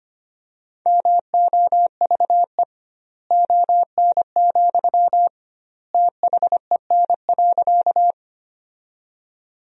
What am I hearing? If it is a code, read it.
Morse code, heard as a keyed tone, about 25 wpm, 700 Hz. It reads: MOVE ON, THEN.
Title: 2x Word-Spacing - Sentences from Top 200 Words - 25wpm